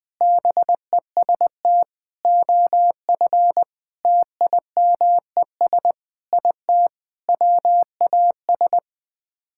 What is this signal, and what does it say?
Morse code, heard as a keyed tone, about 20 wpm, 700 Hz. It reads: BEST OF TIMES IT WAS
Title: Sets of 8 Words - 20wpm